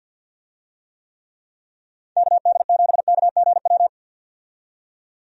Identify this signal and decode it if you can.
Morse code, heard as a keyed tone, about 48 wpm, 700 Hz. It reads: KD7KGW
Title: USA Call Signs - 6 Characters - 50wpm